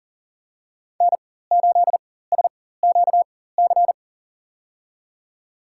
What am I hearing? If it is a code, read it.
Morse code, heard as a keyed tone, about 40 wpm, spaced as for 19 wpm, 700 Hz. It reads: N8SQC
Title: USA Call Signs - 5 Characters - 4x Character Spacing - 40wpm